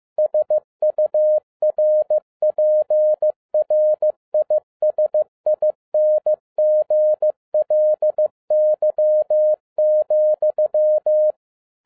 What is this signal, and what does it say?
Morse code, heard as a keyed tone, about 15 wpm, 610 Hz. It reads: SURPRISINGLY,
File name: 2011-07-mp3 / n47 JOHN K3TN 15wpm